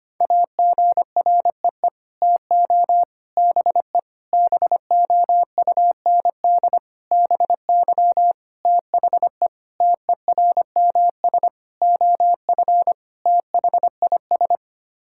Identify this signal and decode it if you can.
Morse code, heard as a keyed tone, about 25 wpm, 700 Hz. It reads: AGREE TO BE BOUND BY THE TERMS OF THIS
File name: Sets of 9 Words - 25wpm